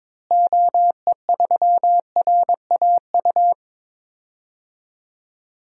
Morse code, heard as a keyed tone, about 22 wpm, 700 Hz. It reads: OE3RAU
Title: DXCC-100 Call Signs - 6 Characters - Repeated 3x - 22wpm